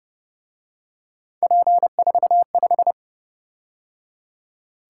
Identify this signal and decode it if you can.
Morse code, heard as a keyed tone, about 29 wpm, 700 Hz. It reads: P45